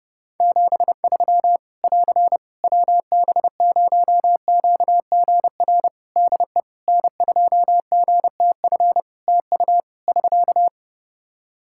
Morse code, heard as a keyed tone, about 30 wpm, 700 Hz. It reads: 73 <AR> WB0QGR DE N2GTF TU <SK>
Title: Practice QSO - Exchange 4 - Ending - 30wpm